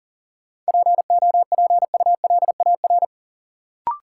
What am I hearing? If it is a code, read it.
Morse code, heard as a keyed tone, about 38 wpm, 700 Hz. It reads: POPULAR
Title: Sentences from Top 1200 Words - New Word Drill Alt - 40wpm